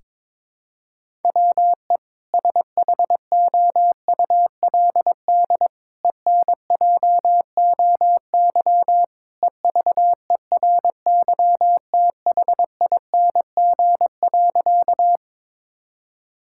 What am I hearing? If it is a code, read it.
Morse code, heard as a keyed tone, about 22 wpm, 700 Hz. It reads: WE SHOULD ENJOY EVERYTHING.
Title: Sentences from Top 1000 Words - Review Part 5 - 22wpm